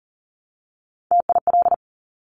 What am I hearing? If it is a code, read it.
Morse code, heard as a keyed tone, about 41 wpm, 700 Hz. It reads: TIL
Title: Top Three Letter Combinations in Words 40wpm